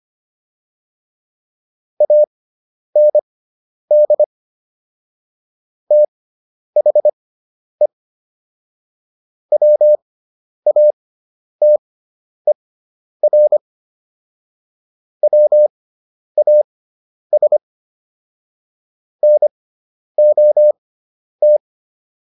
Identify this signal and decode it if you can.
Morse code, heard as a keyed tone, about 25 wpm, spaced as for 10 wpm, 600 Hz. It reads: AND THE WATER WAS NOT